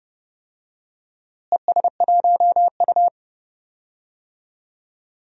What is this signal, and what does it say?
Morse code, heard as a keyed tone, about 32 wpm, 700 Hz. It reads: ES1U
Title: DXCC-100 Call Signs - 4 Characters - Repeated 2x - 30wpm